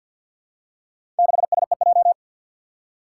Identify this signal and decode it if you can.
Morse code, heard as a keyed tone, about 50 wpm, 700 Hz. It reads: 6SEJ
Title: Four Challenging Letter-Number Combinations 50wpm